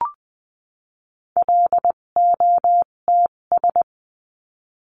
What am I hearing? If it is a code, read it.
Morse code, heard as a keyed tone, about 20 wpm, spaced as for 17 wpm, 700 Hz. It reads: LOTS
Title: 257 Lesson 42b - Words 14-20wpm